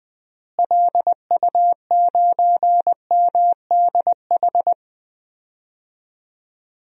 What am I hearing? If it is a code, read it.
Morse code, heard as a keyed tone, about 20 wpm, 700 Hz. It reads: LU9MDH
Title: DXCC-100 Call Signs - 6 Characters - Repeated 2x - 20wpm